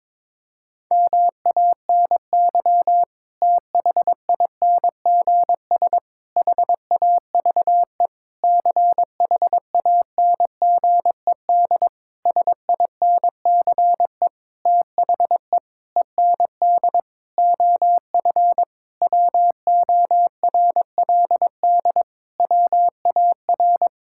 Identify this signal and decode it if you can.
Morse code, heard as a keyed tone, about 22 wpm, 700 Hz. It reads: MANY THINGS HAVE CHANGED SINCE THE END OF WORLD WAR
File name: Sentences with 7-Word Phrases - 22wpm